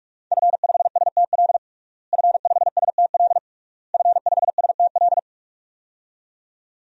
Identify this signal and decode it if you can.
Morse code, heard as a keyed tone, about 41 wpm, 700 Hz. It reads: F5STL F5STL F5STL
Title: DXCC-100 Call Signs - 5 Characters - Repeated 3x - 45wpm